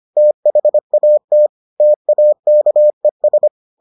Morse code, heard as a keyed tone, about 25 wpm, 600 Hz. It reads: THAT TAKES